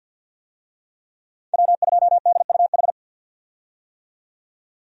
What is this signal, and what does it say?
Morse code, heard as a keyed tone, about 47 wpm, 700 Hz. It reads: W2DUH
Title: USA Call Signs - 5 Characters - 50wpm